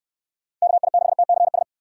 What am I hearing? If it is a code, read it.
Morse code, heard as a keyed tone, about 66 wpm, 700 Hz. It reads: KI6TCS